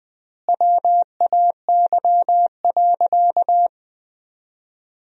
Morse code, heard as a keyed tone, about 20 wpm, 700 Hz. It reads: WAY.
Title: 4x Word-Spacing - Sentences from Top 100 Words - 20wpm